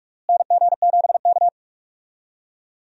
Morse code, heard as a keyed tone, about 43 wpm, 700 Hz. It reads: NG7K
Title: USA Call Signs - 4 Characters - Repeated 2x - 45wpm